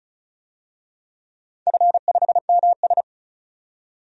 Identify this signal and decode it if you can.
Morse code, heard as a keyed tone, about 36 wpm, 690 Hz. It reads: F5MS
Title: DXCC-100 Call Signs - 4 Characters - 35wpm